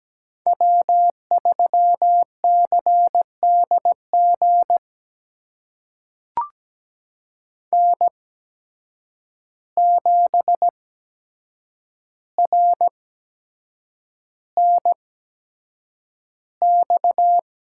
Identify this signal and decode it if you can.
Morse code, heard as a keyed tone, about 17 wpm, spaced as for 7 wpm, 700 Hz. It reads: W3CDG N7RNX
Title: USA Call Signs - 8x Character Spacing - 17wpm